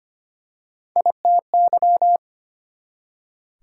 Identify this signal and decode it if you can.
Morse code, heard as a keyed tone, about 25 wpm, 700 Hz. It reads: ITY
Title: Prefixes and Suffixes 25wpm